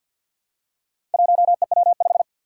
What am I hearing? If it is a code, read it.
Morse code, heard as a keyed tone, about 46 wpm, 700 Hz. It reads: 1EW5
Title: Four Challenging Letter-Number Combinations 50wpm